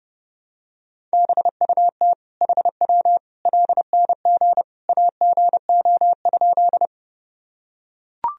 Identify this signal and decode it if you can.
Morse code, heard as a keyed tone, about 30 wpm, 700 Hz. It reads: BUT HW LNG AGO?
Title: Sentences from Top 500 Words and Abbreviations 30wpm